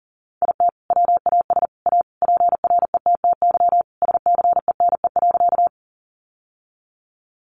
Morse code, heard as a keyed tone, about 40 wpm, 700 Hz. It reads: IT WAS A PRETTY SCENE.
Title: Sentences from Top 1100 Words 40wpm